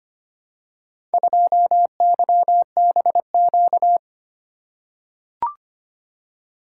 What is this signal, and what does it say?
Morse code, heard as a keyed tone, about 25 wpm, 700 Hz. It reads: 2YBQ
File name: practice / 25wpm / Four Letter-Number Combinations - 25wpm